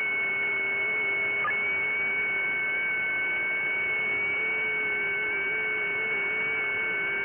FT8.
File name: IC-7100